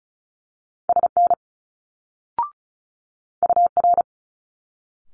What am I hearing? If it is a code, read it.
Morse code, heard as a keyed tone, about 36 wpm, 700 Hz. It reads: SN UR
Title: Multi-letter Phonemes Quick Fire 35wpm